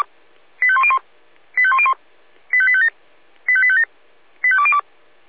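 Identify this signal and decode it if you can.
Five-tone selective call (ZVEI), heard as quick runs of five tones, tones of about 70 ms. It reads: 85101 85101 85857 85857 85202
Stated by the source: ZVEI five-tone selective call 85810 received by SDR.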